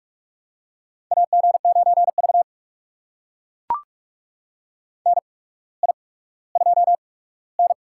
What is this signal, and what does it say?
Morse code, heard as a keyed tone, about 44 wpm, spaced as for 19 wpm, 690 Hz. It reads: AG9V NI2N
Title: USA Call Signs - 4 Characters - 8x Character Spacing - 45wpm